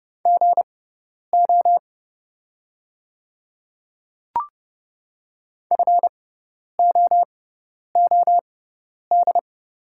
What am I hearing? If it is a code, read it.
Morse code, heard as a keyed tone, about 30 wpm, spaced as for 10 wpm, 690 Hz. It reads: GO FOOD
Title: 6x Character Spacing - Top 500 Words - 30wpm